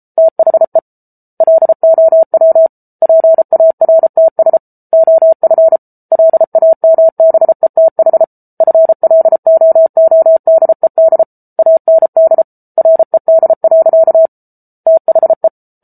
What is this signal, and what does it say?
Morse code, heard as a keyed tone, about 34 wpm, 650 Hz. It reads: THE LOW PARTS OF LAMBETH FLOODED AND RED. THE